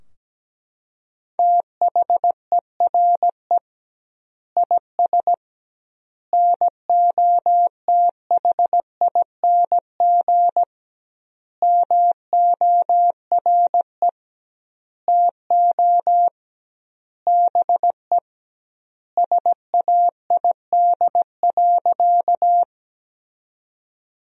Morse code, heard as a keyed tone, about 17 wpm, 700 Hz. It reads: THERE IS NOTHING MORE TO BE SAID.
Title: 2x Word-Spacing - Sentences from Top 500 Words - 17wpm